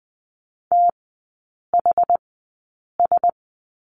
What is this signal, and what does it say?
Morse code, heard as a keyed tone, about 20 wpm, spaced as for 8 wpm, 700 Hz. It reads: THS
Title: 127 Lesson 21e - Number 6 - Three Characters 20wpm